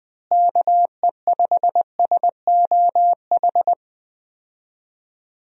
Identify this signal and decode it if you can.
Morse code, heard as a keyed tone, about 20 wpm, 700 Hz. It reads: KE5SOH
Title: USA Call Signs - 6 Characters - Repeated 3x - 20wpm